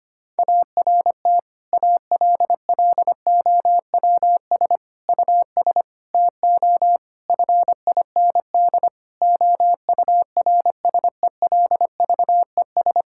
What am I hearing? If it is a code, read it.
Morse code, heard as a keyed tone, about 25 wpm, 700 Hz. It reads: ART ALLOWS US TO FIND OURSELVES
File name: practice / 25wpm / Difficult Sentences with Hints - 25wpm